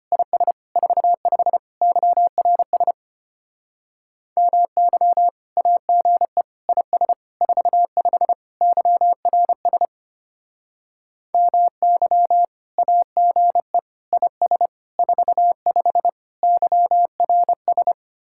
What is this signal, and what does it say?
Morse code, heard as a keyed tone, about 30 wpm, 700 Hz. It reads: IS 45 YRS MY AGE IS 45 YRS MY AGE IS 45 YRS
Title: Speed-Racing Common QSO Phrases Encore - 25wpm